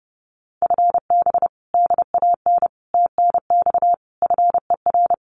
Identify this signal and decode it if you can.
Morse code, heard as a keyed tone, about 30 wpm, 700 Hz. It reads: FB DAN TNX FER
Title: Common QSO Phrases Encore 30wpm